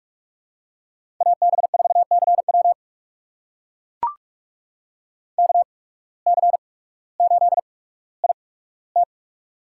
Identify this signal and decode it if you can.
Morse code, heard as a keyed tone, about 44 wpm, spaced as for 19 wpm, 700 Hz. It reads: AB4CW KC8IT
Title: USA Call Signs - 5 Characters - 8x Character Spacing - 45wpm